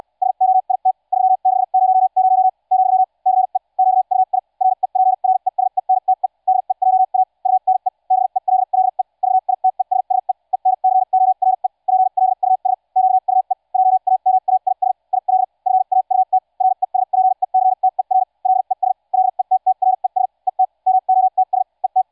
Morse code, heard as a keyed tone, about 74 wpm, spaced as for 52 wpm, 750 Hz. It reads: SOIT 3600 , CE QUI REPRESENTE DEJA UNE BELLE VITESSE ENCORE PLUS QUE JAMAIS IL FAUT RESPECTER LES LETTRES ET MOTS ET